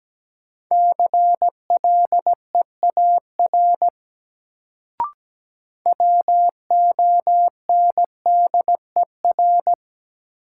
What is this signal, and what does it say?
Morse code, heard as a keyed tone, about 17 wpm, 700 Hz. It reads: CLEAR WONDER
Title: Sentences from Top 500 Words - New Word Drill - 17wpm